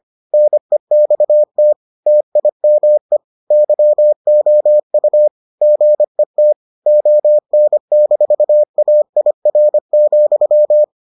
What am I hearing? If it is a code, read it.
Morse code, heard as a keyed tone, about 25 wpm, 600 Hz. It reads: NEXT TIME YOU GET ON-AIR,